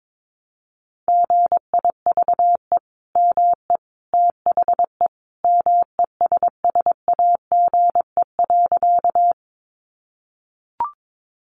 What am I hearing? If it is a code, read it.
Morse code, heard as a keyed tone, about 22 wpm, 700 Hz. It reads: GIVE ME THE MESSAGE.
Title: Sentences from Top 1100 Words 22wpm